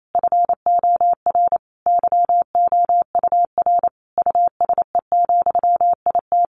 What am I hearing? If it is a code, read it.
Morse code, heard as a keyed tone, about 28 wpm, 700 Hz. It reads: FOR YOUR USE, IT
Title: A Sound Of Thunder 28wpm